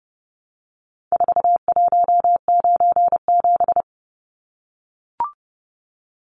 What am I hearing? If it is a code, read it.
Morse code, heard as a keyed tone, about 30 wpm, 700 Hz. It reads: 4197
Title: Numbers - 4-Digit 30wpm